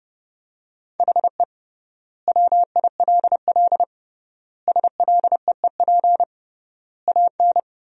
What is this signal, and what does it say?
Morse code, heard as a keyed tone, about 30 wpm, 700 Hz. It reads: HE WILL SLEEP AN
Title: 3x Word-Spacing - Sentences from Top 500 Words - 30wpm